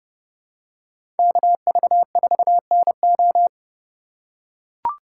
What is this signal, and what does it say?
Morse code, heard as a keyed tone, about 30 wpm, 700 Hz. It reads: KV4NO
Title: USA Call Signs - 5 Characters - 2x Character Spacing - 30wpm